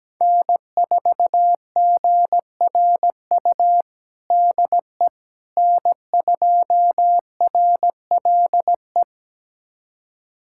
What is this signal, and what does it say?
Morse code, heard as a keyed tone, about 17 wpm, 700 Hz. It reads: N4GRU DE N2RLE
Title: USA Callsign DE Callsign - 5 Characters - 17wpm